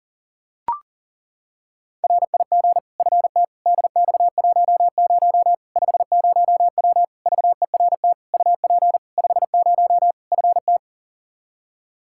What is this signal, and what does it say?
Morse code, heard as a keyed tone, about 40 wpm, 700 Hz. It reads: RIG FT DX10 50W VERT UP 50 FT
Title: Practice QSO - Exchange 2 - Info - 40wpm